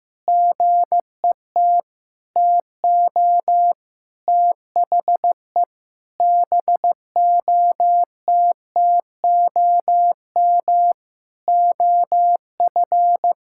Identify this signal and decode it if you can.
Morse code, heard as a keyed tone, about 16 wpm, 700 Hz. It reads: GET TO THE BOTTOM OF
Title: Sentences from Top 1000 Words - Review Part 6 - 15wpm